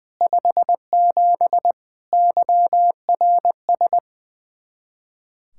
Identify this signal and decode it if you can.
Morse code, heard as a keyed tone, about 20 wpm, 700 Hz. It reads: 57 YRS
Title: Speed-Racing Common QSO Phrases Encore - 20wpm